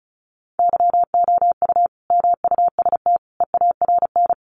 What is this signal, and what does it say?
Morse code, heard as a keyed tone, about 34 wpm, 700 Hz. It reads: YOU MUST EARN
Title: Sentences from Top 1500 Words 35wpm